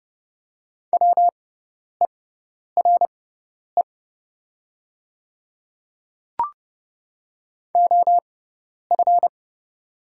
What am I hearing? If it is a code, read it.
Morse code, heard as a keyed tone, about 30 wpm, spaced as for 10 wpm, 700 Hz. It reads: WERE OF
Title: 6x Character Spacing - Top 100 Words - 30wpm